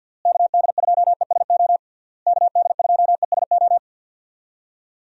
Morse code, heard as a keyed tone, about 52 wpm, 700 Hz. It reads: KD2ESO KD2ESO
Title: USA Call Signs - 6 Characters - Repeated 3x - 50wpm